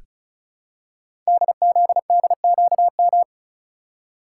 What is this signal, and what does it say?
Morse code, heard as a keyed tone, about 36 wpm, 700 Hz. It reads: DZDQM
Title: Code Groups - 5 Letters - 35wpm